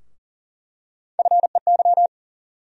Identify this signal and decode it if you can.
Morse code, heard as a keyed tone, about 40 wpm, 700 Hz. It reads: FEY
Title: USA Call Sign Postfixes - 3 Letters 40wpm